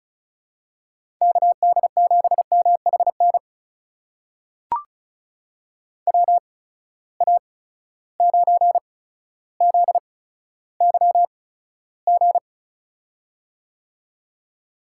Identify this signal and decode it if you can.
Morse code, heard as a keyed tone, about 36 wpm, spaced as for 15 wpm, 700 Hz. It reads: KD7MHN WA9ZYG
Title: USA Call Signs - 6 Characters - 8x Character Spacing - 35wpm